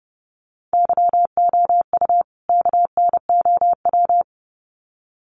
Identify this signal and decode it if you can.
Morse code, heard as a keyed tone, about 30 wpm, 700 Hz. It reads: YOU KNOW
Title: Sets of 2 Words 30wpm